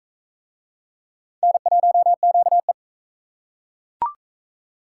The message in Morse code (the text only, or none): N1QE